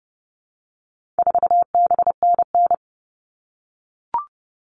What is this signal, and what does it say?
Morse code, heard as a keyed tone, about 30 wpm, 700 Hz. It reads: 4BNN
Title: All Cut Numbers - 4-Digit 30wpm